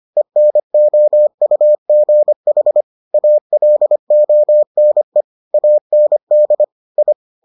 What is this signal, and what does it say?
Morse code, heard as a keyed tone, about 25 wpm, 600 Hz. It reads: ENOUGH ALONE AND I